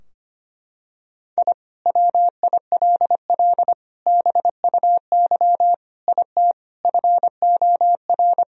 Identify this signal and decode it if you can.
Morse code, heard as a keyed tone, about 25 wpm, 700 Hz. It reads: I WILL BUY IT FOR
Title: Sentences from Top 1000 Words - Review Part 4 - 25wpm